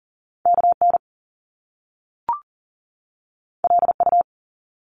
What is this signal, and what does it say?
Morse code, heard as a keyed tone, about 41 wpm, 700 Hz. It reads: KN LU
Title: Top Two Letter Combinations in Words 40wpm